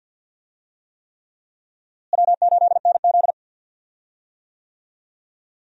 Morse code, heard as a keyed tone, about 50 wpm, 700 Hz. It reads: W8NZ
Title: USA Call Signs - 4 Characters - Repeated 3x - 50wpm